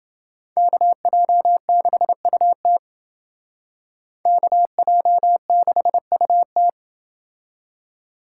Morse code, heard as a keyed tone, about 29 wpm, 700 Hz. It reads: KJ6UT KJ6UT